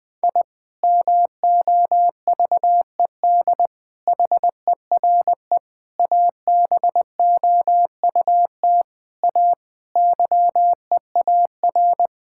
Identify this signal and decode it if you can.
Morse code, heard as a keyed tone, about 21 wpm, 700 Hz. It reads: I MOVED HERE ABOUT A YEAR
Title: Sentences with 7-Word Phrases - 20wpm